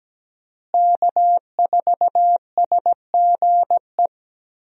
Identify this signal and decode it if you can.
Morse code, heard as a keyed tone, about 17 wpm, 700 Hz. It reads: K4SGE
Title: USA Call Signs - 5 Characters - Repeated 3x - 17wpm